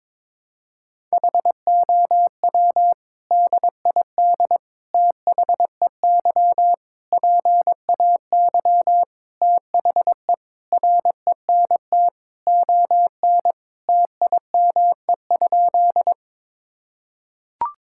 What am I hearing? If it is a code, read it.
Morse code, heard as a keyed tone, about 22 wpm, 700 Hz. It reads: HOW DID THEY PAY THE RENT ON TIME?
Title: Sentences from Top 1000 Words - Review Part 5 - 22wpm